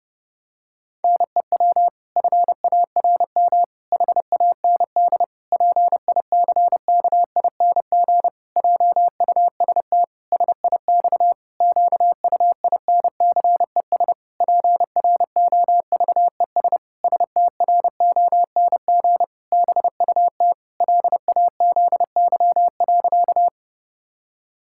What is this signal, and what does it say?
Morse code, heard as a keyed tone, about 32 wpm, 700 Hz. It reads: NEW FARM HAND PICKING JUST SIX QUINCES PROVES STRONG BUT LAZY.
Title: Pangram Challenge v2 - 30wpm